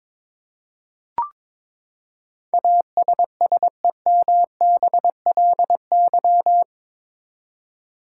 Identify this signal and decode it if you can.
Morse code, heard as a keyed tone, about 22 wpm, 700 Hz. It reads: ASSEMBLY
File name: practice / 22wpm / Eight Letter Words - 22wpm